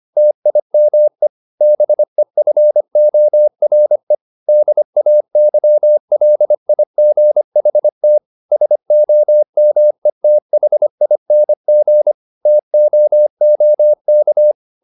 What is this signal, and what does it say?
Morse code, heard as a keyed tone, about 25 wpm, 600 Hz. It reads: TIME BEFORE DAYLIGHT SOMETHING TOOK
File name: the old man and the sea part 1 25wpm